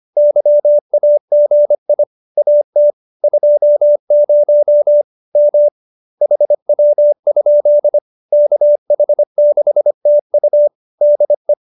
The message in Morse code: YAGI AT 20 M HW? KH6TU DE

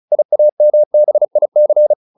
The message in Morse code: IAMBIC

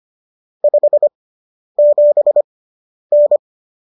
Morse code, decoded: 57N

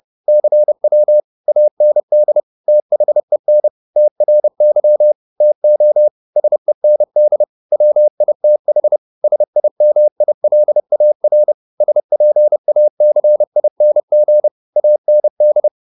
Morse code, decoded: CW AND THEN TRY TO SEND WITH SIMILAR SPACING AND

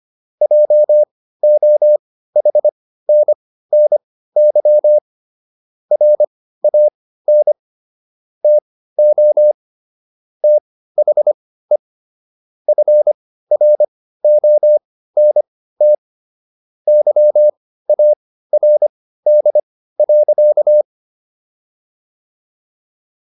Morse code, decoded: JOHNNY RAN TO THE FRONT YARD.

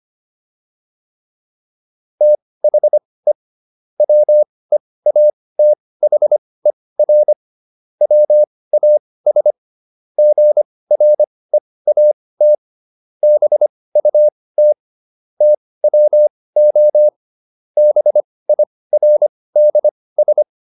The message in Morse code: THE WEATHER WAS GREAT BUT TWO BIRDS

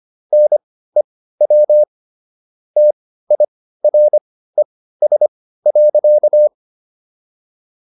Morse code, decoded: NEW TIRES.

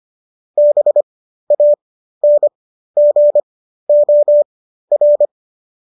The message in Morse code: BANGOR

